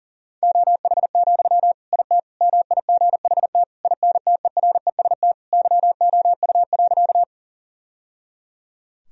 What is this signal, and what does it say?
Morse code, heard as a keyed tone, about 40 wpm, 700 Hz. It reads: OH, IT MIGHT INTEREST YOU.
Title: Sentences from Top 1000 Words - Review Part 2 - 40wpm